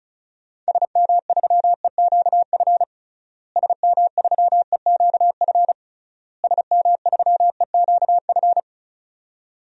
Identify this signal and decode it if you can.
Morse code, heard as a keyed tone, about 36 wpm, 700 Hz. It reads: SM3EQF SM3EQF SM3EQF
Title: DXCC-100 Call Signs - 6 Characters - Repeated 3x - 35wpm